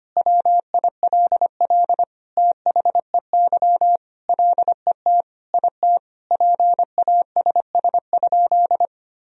Morse code, heard as a keyed tone, about 25 wpm, 700 Hz. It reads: WILL THEY LET IT PASS?
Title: Sentences from Top 400 Words and Abbreviations 25wpm